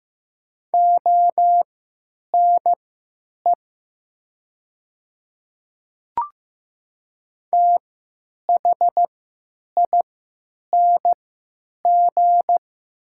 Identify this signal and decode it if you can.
Morse code, heard as a keyed tone, about 15 wpm, spaced as for 9 wpm, 700 Hz. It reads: ONE THING
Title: 3x Character Spacing - Top 100 Words - 15wpm